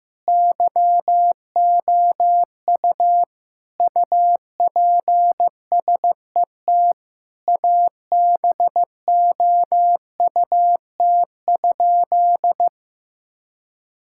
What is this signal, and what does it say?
Morse code, heard as a keyed tone, about 15 wpm, 700 Hz. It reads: YOU UPSET ABOUT?
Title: Sentences from Top 1000 Words - Review Part 5 - 15wpm